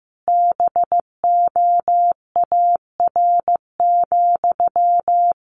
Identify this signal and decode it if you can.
Morse code, heard as a keyed tone, about 15 wpm, 700 Hz. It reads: BOAR,
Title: A Sound Of Thunder 15wpm